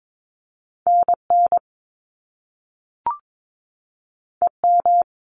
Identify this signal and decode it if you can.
Morse code, heard as a keyed tone, about 22 wpm, 700 Hz. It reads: NN EM
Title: Top Two Letter Combinations in Words 22wpm